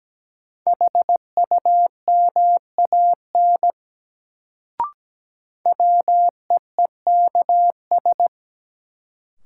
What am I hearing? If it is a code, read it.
Morse code, heard as a keyed tone, about 17 wpm, 700 Hz. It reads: HUMAN WEEKS
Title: Sets of 5 Words - New Word Drill - 17wpm